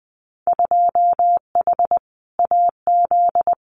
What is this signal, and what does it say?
Morse code, heard as a keyed tone, about 20 wpm, 700 Hz. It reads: 2H AZ
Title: Winter Field Day Exchanges 20wpm